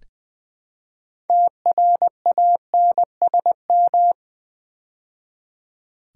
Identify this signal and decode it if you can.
Morse code, heard as a keyed tone, about 20 wpm, 700 Hz. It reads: TRANSM
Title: Build A Word - 2 Letters At A Time - 8-Letter Words - 20wpm